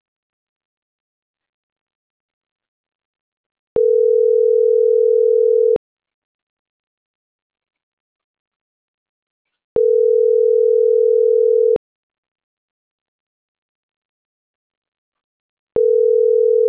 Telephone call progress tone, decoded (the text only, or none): ringback tone